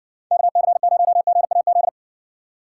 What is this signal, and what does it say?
Morse code, heard as a keyed tone, about 59 wpm, 700 Hz. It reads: KC0GAZ